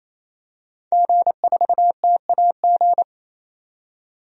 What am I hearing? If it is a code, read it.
Morse code, heard as a keyed tone, about 28 wpm, 700 Hz. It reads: G4TAG
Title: DXCC-100 Call Signs - 5 Characters - 28wpm